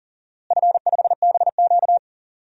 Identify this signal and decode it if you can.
Morse code, heard as a keyed tone, about 40 wpm, 700 Hz. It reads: F5BQ
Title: DXCC-100 Call Signs - 4 Characters - Repeated 2x - 40wpm